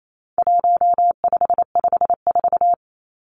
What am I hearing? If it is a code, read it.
Morse code, heard as a keyed tone, about 28 wpm, 700 Hz. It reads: 1554
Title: Numbers - 4-Digit 28wpm